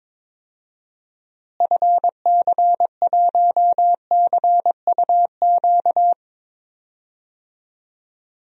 Morse code, heard as a keyed tone, about 22 wpm, 700 Hz. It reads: FC1CUQ
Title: DXCC-100 Call Signs - 6 Characters - Repeated 3x - 22wpm